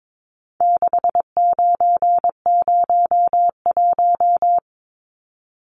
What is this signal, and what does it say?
Morse code, heard as a keyed tone, about 22 wpm, 700 Hz. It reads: 6901
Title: Numbers - 4-Digit 22wpm